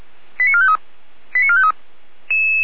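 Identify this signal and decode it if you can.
Five-tone selective call (ZVEI), heard as quick runs of five tones, tones of about 70 ms. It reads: 89453 89453
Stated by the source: ZVEI five-tone call received by SDR.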